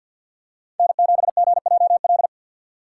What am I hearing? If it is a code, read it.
Morse code, heard as a keyed tone, about 52 wpm, 690 Hz. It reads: N7GJL
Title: USA Call Signs - 5 Characters - 2x Character Spacing - 50wpm